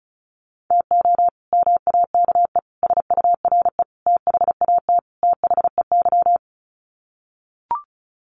Morse code, heard as a keyed tone, about 34 wpm, 700 Hz. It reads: TO MAKE SURE THAT THEY
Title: Sets of 5 Words Encore2 35wpm